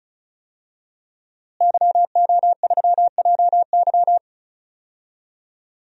Morse code, heard as a keyed tone, about 36 wpm, 700 Hz. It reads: YO3JY
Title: DXCC-100 Call Signs - 5 Characters - Repeated 3x - 35wpm